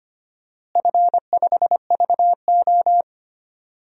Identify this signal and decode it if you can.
Morse code, heard as a keyed tone, about 25 wpm, 700 Hz. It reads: F5VO
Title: DXCC-100 Call Signs - 4 Characters - Repeated 3x - 25wpm